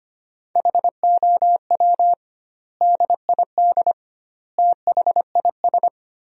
Morse code, heard as a keyed tone, about 25 wpm, 700 Hz. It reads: HOW DID THIS
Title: 2x Word-Spacing - Sentences from Top 400 Words - 25wpm